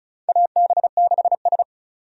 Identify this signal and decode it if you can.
Morse code, heard as a keyed tone, about 36 wpm, 700 Hz. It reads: AB6S
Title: USA Call Signs - 4 Characters - 35wpm